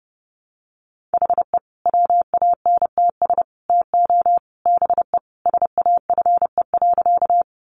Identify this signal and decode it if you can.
Morse code, heard as a keyed tone, about 30 wpm, 700 Hz. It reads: HE WANTS TO BE SAFE.
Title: Additional Easy Phrases 30wpm